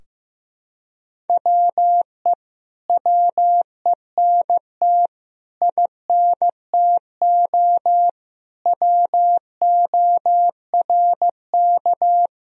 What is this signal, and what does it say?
Morse code, heard as a keyed tone, about 15 wpm, 700 Hz. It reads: WE WENT INTO WORK
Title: Sentences from Top 1000 Words - Review Part 4 - 15wpm